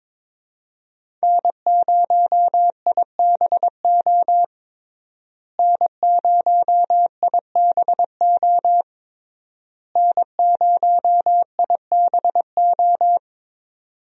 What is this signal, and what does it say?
Morse code, heard as a keyed tone, about 22 wpm, 700 Hz. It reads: N0IBO N0IBO N0IBO
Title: USA Call Signs - 5 Characters - Repeated 3x - 22wpm